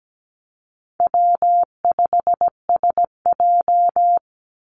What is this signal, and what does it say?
Morse code, heard as a keyed tone, about 17 wpm, 700 Hz. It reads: W5SJ
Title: Call Signs Repeated 3x - 17wpm